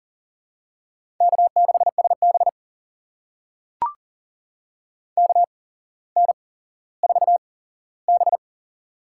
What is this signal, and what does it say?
Morse code, heard as a keyed tone, about 40 wpm, spaced as for 17 wpm, 690 Hz. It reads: K6SB KN4B
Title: USA Call Signs - 4 Characters - 8x Character Spacing - 40wpm